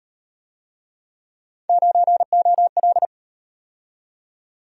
Morse code, heard as a keyed tone, about 38 wpm, 700 Hz. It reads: 9OL